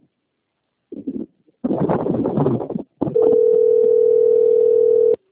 Ringback tone (North America), 440 + 480 Hz.
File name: out- - - - - .26726